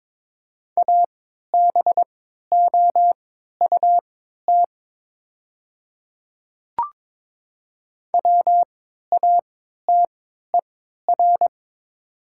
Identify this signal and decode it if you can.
Morse code, heard as a keyed tone, about 22 wpm, spaced as for 13 wpm, 700 Hz. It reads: ABOUT WATER
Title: 3x Character Spacing - Top 100 Words - 22wpm